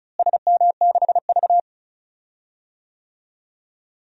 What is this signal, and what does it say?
Morse code, heard as a keyed tone, about 34 wpm, 700 Hz. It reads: SM6V
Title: DXCC-100 Call Signs - 4 Characters - Repeated 3x - 35wpm